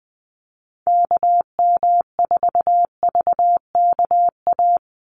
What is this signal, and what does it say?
Morse code, heard as a keyed tone, about 20 wpm, 700 Hz. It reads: KM4VKA
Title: USA Call Signs 20wpm